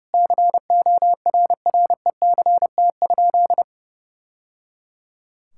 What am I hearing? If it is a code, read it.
Morse code, heard as a keyed tone, about 30 wpm, 700 Hz. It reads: CORRECT?
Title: Sentences from Top 1000 Words - Review Part 6 - 30wpm